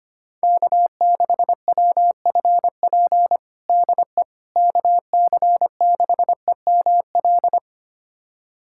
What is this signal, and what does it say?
Morse code, heard as a keyed tone, about 25 wpm, 700 Hz. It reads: K6WFP DE KC6EML
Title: USA Callsign DE Callsign 25wpm